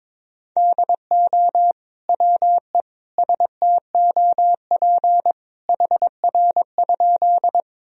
Morse code, heard as a keyed tone, about 21 wpm, 700 Hz. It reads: DO WE STOP HR?
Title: Sentences from Top 300 Words and Abbreviations 22wpm